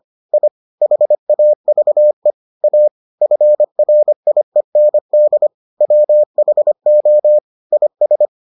Morse code, heard as a keyed tone, about 25 wpm, 600 Hz. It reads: I HAVE A FRIEND WHO IS